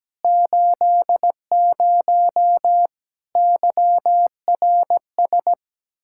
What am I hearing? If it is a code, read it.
Morse code, heard as a keyed tone, about 17 wpm, 700 Hz. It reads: 80 YRS
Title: Practice QSO - Exchange 3 - More Info - 17wpm